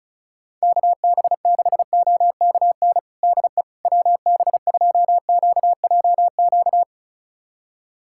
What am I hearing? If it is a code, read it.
Morse code, heard as a keyed tone, about 36 wpm, 700 Hz. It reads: KB6OKN DE WB2QJQ
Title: USA Callsign DE Callsign - 6 Characters - 35wpm